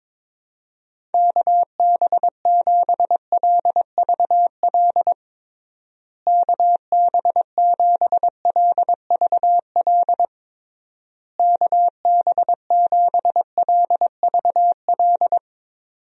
Morse code, heard as a keyed tone, about 22 wpm, 700 Hz. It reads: KB7LVL KB7LVL KB7LVL
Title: USA Call Signs - 6 Characters - Repeated 3x - 22wpm